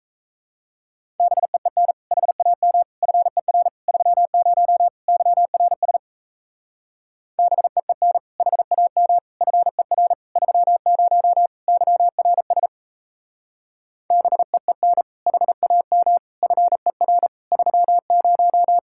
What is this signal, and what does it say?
Morse code, heard as a keyed tone, about 37 wpm, 700 Hz. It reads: BEEN HAM FER 30 YRS BEEN HAM FER 30 YRS BEEN HAM FER 30